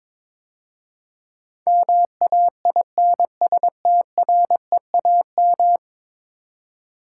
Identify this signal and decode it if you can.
Morse code, heard as a keyed tone, about 22 wpm, 700 Hz. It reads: MAINSTREAM
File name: practice / 22wpm / 10 Letter Words - First 3 Letters Given - 22wpm